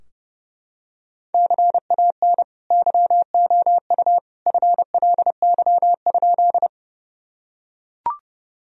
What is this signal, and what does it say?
Morse code, heard as a keyed tone, about 30 wpm, 700 Hz. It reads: CAN YOU FLY?
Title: Sentences from Top 1000 Words - Review Part 2 - 30wpm